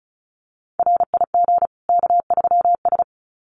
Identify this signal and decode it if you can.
Morse code, heard as a keyed tone, about 34 wpm, 700 Hz. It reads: RIG K3S
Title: Common QSO Phrases 35wpm